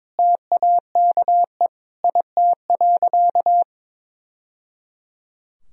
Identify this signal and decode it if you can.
Morse code, heard as a keyed tone, about 22 wpm, 700 Hz. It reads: TAKE IT.
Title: Sentences from Top 1000 Words - Review Part 4 - 22wpm